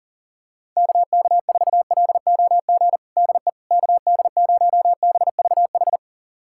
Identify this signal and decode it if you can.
Morse code, heard as a keyed tone, about 40 wpm, 700 Hz. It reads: KK4LOG DE KD0BVH
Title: USA Callsign DE Callsign - 6 Characters - 40wpm